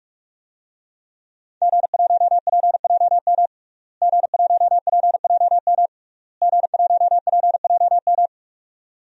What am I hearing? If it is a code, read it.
Morse code, heard as a keyed tone, about 46 wpm, 700 Hz. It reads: G1PJM G1PJM G1PJM
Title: DXCC-100 Call Signs - 5 Characters - Repeated 3x - 45wpm